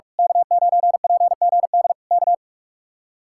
Morse code, heard as a keyed tone, about 41 wpm, 700 Hz. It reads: K9PGD K
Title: Speed-Racing Common QSO Phrases - 45wpm